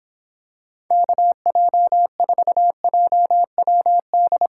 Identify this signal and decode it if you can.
Morse code, heard as a keyed tone, about 26 wpm, 700 Hz. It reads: KJ4JWD